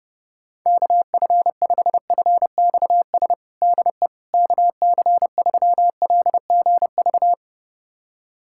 Morse code, heard as a keyed tone, about 32 wpm, 700 Hz. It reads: KF5FXS DE KC3LGV
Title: USA Callsign DE Callsign - 6 Characters - 30wpm